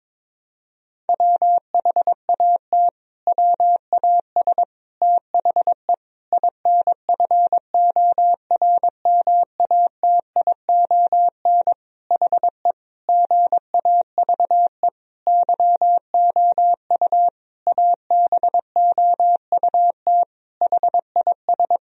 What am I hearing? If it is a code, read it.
Morse code, heard as a keyed tone, about 22 wpm, 700 Hz. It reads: WHAT WAS THE INFORMATION HE GAVE YOU ABOUT HIS
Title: Sentences from Top 1000 Words - Review Part 3 - 22wpm